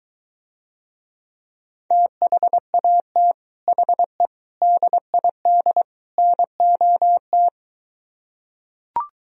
Morse code, heard as a keyed tone, about 23 wpm, 700 Hz. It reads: THAT HE DID NOT